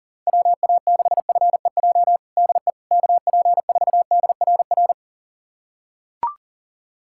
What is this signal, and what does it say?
Morse code, heard as a keyed tone, about 40 wpm, 700 Hz. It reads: WA6FEJ DE KP4DRR
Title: USA Callsign DE Callsign - 6 Characters - 40wpm